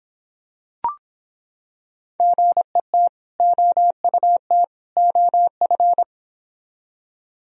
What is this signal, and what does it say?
Morse code, heard as a keyed tone, about 25 wpm, 700 Hz. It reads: GET OUT OF